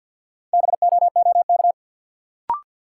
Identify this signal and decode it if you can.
Morse code, heard as a keyed tone, about 48 wpm, 700 Hz. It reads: BOOK
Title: Sentences from Top 300 Words - New Word Drill - 50wpm